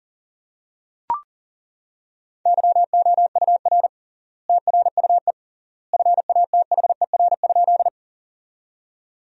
Morse code, heard as a keyed tone, about 40 wpm, 700 Hz. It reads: YOUR TRUE FATHER?
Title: 3x Word-Spacing - Sentences from Top 400 Words - 40wpm